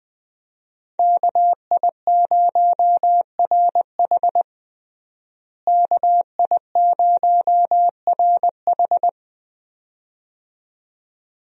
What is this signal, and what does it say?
Morse code, heard as a keyed tone, about 20 wpm, 700 Hz. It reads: KI0RH KI0RH
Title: USA Call Signs - 5 Characters - Repeated 2x - 20wpm